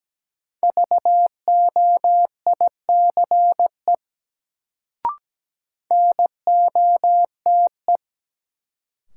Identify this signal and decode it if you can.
Morse code, heard as a keyed tone, about 17 wpm, 700 Hz. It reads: VOICE NOTE
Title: Sentences from Top 500 Words - New Word Drill - 17wpm